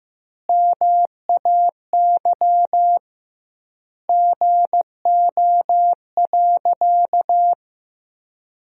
Morse code, heard as a keyed tone, about 15 wpm, 700 Hz. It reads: MAY GO.
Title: 2x Word-Spacing - Sentences from Top 100 Words - 15wpm